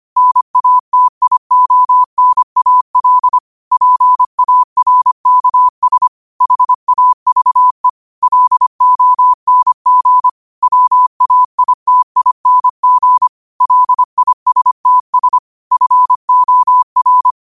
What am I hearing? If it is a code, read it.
Morse code, heard as a keyed tone, about 25 wpm, 1000 Hz. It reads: NATIONAL PARKS HAVE LONG WAITING LISTS FOR
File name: qotdfiles / 25wpm